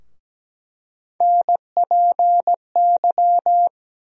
Morse code, heard as a keyed tone, about 17 wpm, 700 Hz. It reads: NPY
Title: USA Call Sign Postfixes - 3 Letters 17wpm